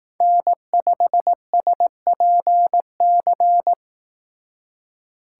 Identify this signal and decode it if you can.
Morse code, heard as a keyed tone, about 18 wpm, 700 Hz. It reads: N5SPC
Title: Speed Racing - USA 1x3 Call Signs 15wpm